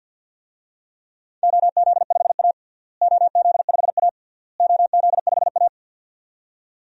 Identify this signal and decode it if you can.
Morse code, heard as a keyed tone, about 51 wpm, 700 Hz. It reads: OZ5A OZ5A OZ5A
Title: DXCC-100 Call Signs - 4 Characters - Repeated 3x - 50wpm